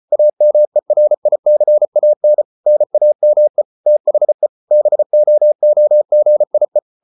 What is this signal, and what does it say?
Morse code, heard as a keyed tone, about 34 wpm, 600 Hz. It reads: AMERICAN NAME THE BOOGIE